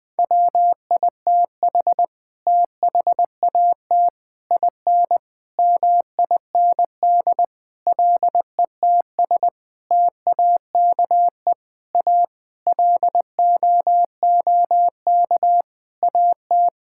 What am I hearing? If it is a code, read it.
Morse code, heard as a keyed tone, about 20 wpm, 700 Hz. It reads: WITH THAT IN MIND LETS TAKE A LOOK AT
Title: Sets of 9 Words - 20wpm